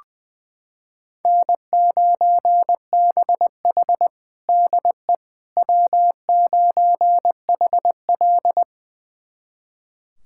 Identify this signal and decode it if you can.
Morse code, heard as a keyed tone, about 20 wpm, 700 Hz. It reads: N9BH DE W9HL
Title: USA Callsign DE Callsign - 4 Characters - 20wpm